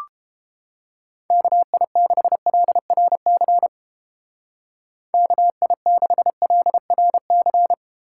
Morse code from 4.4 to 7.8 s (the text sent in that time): KI6LRC